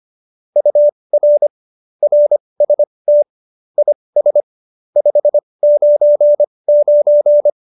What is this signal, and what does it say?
Morse code, heard as a keyed tone, about 25 wpm, spaced as for 20 wpm, 590 Hz. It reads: UR RST IS 599
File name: qso216 20